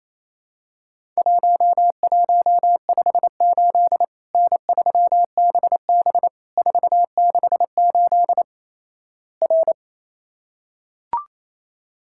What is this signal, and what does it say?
Morse code, heard as a keyed tone, about 28 wpm, 700 Hz. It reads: 1158 N3BB 468 R
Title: Worked All Europe DX Contest - QTC Exchange - 28wpm